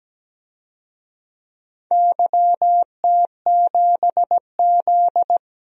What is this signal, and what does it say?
Morse code, heard as a keyed tone, about 17 wpm, 700 Hz. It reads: YT7Z
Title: DXCC-100 Call Signs - 4 Characters - 17wpm